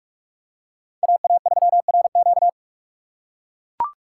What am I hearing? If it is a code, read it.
Morse code, heard as a keyed tone, about 42 wpm, 700 Hz. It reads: AA3RQ
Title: USA Call Signs - 2x Character Spacing - 45wpm